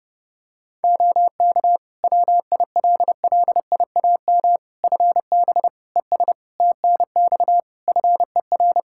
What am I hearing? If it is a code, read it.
Morse code, heard as a keyed tone, about 32 wpm, 700 Hz. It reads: OK WILLIAM FB ES TNX FER
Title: Practice QSO - Exchange 2 - Info - 30wpm